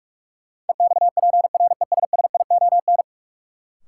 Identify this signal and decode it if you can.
Morse code, heard as a keyed tone, about 46 wpm, 700 Hz. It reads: EXPRESSION
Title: Sentences from Top 1200 Words - New Word Drill - 45wpm